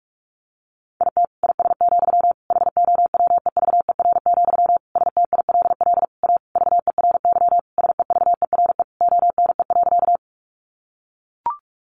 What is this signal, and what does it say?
Morse code, heard as a keyed tone, about 44 wpm, 700 Hz. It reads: IT IS, HOWEVER, STILL A VERY SEVERE ONE.
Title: Sentences from Top 1400 Words 45wpm